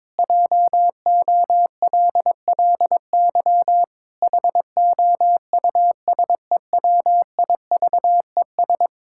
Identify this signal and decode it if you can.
Morse code, heard as a keyed tone, about 22 wpm, 700 Hz. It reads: JOLLY HOUSEWIVES
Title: Pangram Challenge v2 - 22wpm